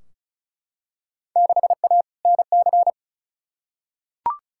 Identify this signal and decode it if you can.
Morse code, heard as a keyed tone, about 36 wpm, 690 Hz. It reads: 6A NC
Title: Field Day Exchanges Encore - 35wpm